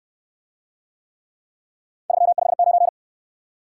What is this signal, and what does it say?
Morse code, heard as a keyed tone, about 68 wpm, 700 Hz. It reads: 359